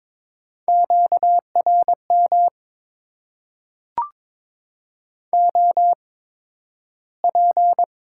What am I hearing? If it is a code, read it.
Morse code, heard as a keyed tone, about 22 wpm, spaced as for 12 wpm, 700 Hz. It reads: QRM OP
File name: Top 90 QSO Elements - 8x Character Spacing - 22wpm